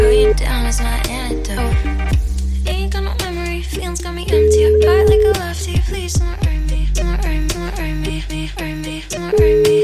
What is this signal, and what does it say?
Ringback tone (Europe, 425 Hz).